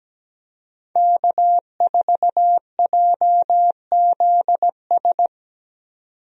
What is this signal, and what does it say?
Morse code, heard as a keyed tone, about 17 wpm, 700 Hz. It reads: K4JZS
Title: USA Call Signs - 5 Characters - 17wpm